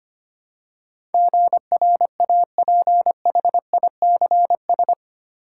Morse code, heard as a keyed tone, about 25 wpm, 690 Hz. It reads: GRAPHICS
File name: 8 Letter Words - First 3 Letters Given - 25wpm